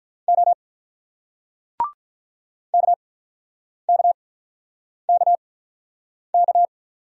Morse code, heard as a keyed tone, about 47 wpm, spaced as for 9 wpm, 700 Hz. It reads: O KKKK